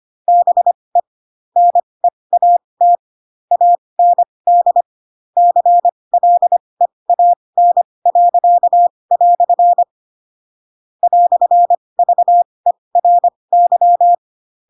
Morse code, heard as a keyed tone, about 25 wpm, spaced as for 20 wpm, 700 Hz. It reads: BE NEAT AND CLEAN." "VERY
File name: Pinocchio / Pinocchio ch8 15 25 ik1hge